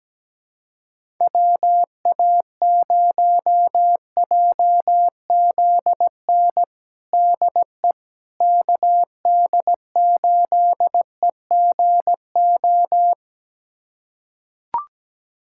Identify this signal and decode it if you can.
Morse code, heard as a keyed tone, about 17 wpm, 700 Hz. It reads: WA0JZN DE KD8EGO
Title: USA Callsign DE Callsign 17wpm